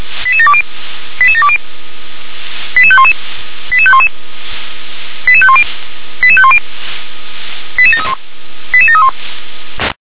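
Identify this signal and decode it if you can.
Five-tone selective call (ZVEI), heard as quick runs of five tones, tones of about 70 ms. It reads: 88720 88720 88511 88511 88510 88510 88731 88731
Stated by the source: ZVEI five-tone call 88730, received by SDR.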